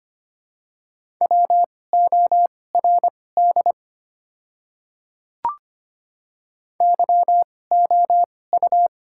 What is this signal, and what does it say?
Morse code, heard as a keyed tone, about 25 wpm, spaced as for 18 wpm, 690 Hz. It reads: WORD YOU
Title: 2x Character Spacing - Top 100 Words - 25wpm